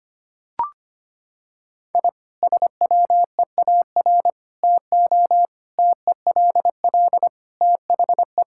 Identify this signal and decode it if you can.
Morse code, heard as a keyed tone, about 25 wpm, 700 Hz. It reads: I SWEAR TO TELL THE
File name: practice / 25wpm / Sentences with 9-Word Phrases - 25wpm